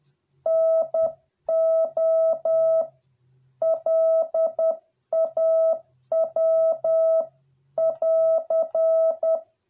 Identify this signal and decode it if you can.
Morse code, heard as a keyed tone, about 10 wpm, 650 Hz. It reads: NO LAW <AR>